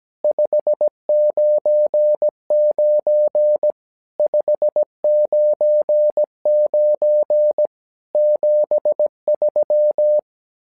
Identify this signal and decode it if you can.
Morse code, heard as a keyed tone, about 17 wpm, 600 Hz. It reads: 599 599 73
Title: SOTA QSO Practice 17wpm